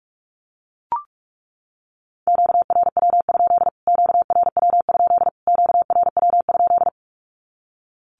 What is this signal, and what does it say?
Morse code, heard as a keyed tone, about 44 wpm, 700 Hz. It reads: QRW? QRW? QRW?
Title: Q-Codes Repeated 3x 45wpm